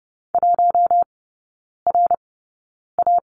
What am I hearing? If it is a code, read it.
Morse code, heard as a keyed tone, about 30 wpm, spaced as for 9 wpm, 700 Hz. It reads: 1RA
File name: Lesson 19e - Letter w - Three Characters 30wpm